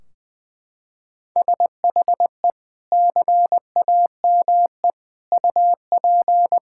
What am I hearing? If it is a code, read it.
Morse code, heard as a keyed tone, about 21 wpm, 700 Hz. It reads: SHE CAME UP
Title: Sentences from Top 1600 Words 20wpm